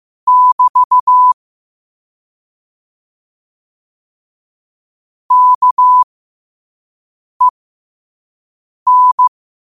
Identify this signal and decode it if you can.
Morse code, heard as a keyed tone, about 15 wpm, spaced as for 5 wpm, 1000 Hz. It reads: <BT> KEN